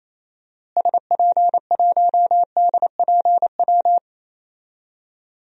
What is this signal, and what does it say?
Morse code, heard as a keyed tone, about 28 wpm, 700 Hz. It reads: SP1DPW
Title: DXCC-100 Call Signs - 6 Characters - Repeated 2x - 28wpm